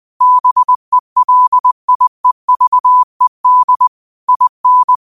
Morse code, heard as a keyed tone, about 20 wpm, 1000 Hz. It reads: BELIEVED IN